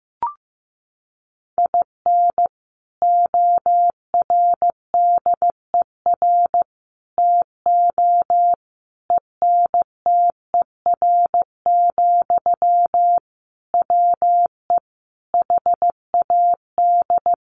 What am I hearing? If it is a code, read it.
Morse code, heard as a keyed tone, about 15 wpm, 700 Hz. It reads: IN ORDER TO ENTER, WE HAD
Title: Sentences from Top 1300 Words 15wpm